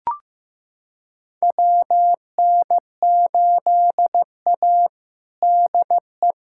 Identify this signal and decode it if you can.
Morse code, heard as a keyed tone, about 15 wpm, 700 Hz. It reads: WN8A DE
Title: USA Callsign DE Callsign - 4 Characters - 15wpm